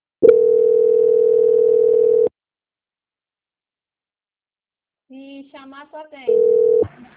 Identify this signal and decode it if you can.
Ringback tone (North America), 440 + 480 Hz.